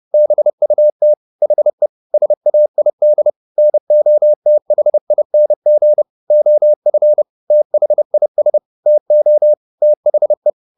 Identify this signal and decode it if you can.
Morse code, heard as a keyed tone, about 30 wpm, 600 Hz. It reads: BUT HE SAID NOTHING OF THIS TO THE
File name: the old man and the sea / the old man and the sea 1 30wpm